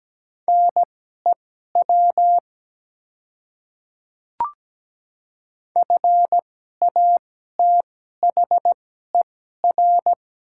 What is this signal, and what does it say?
Morse code, heard as a keyed tone, about 17 wpm, spaced as for 12 wpm, 700 Hz. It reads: NEW FATHER
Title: 2x Character Spacing - Top 200 Words - 17wpm